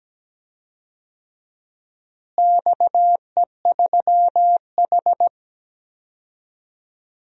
Morse code, heard as a keyed tone, about 17 wpm, 700 Hz. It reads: XE3H
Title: DXCC-100 Call Signs - 4 Characters - Repeated 2x - 17wpm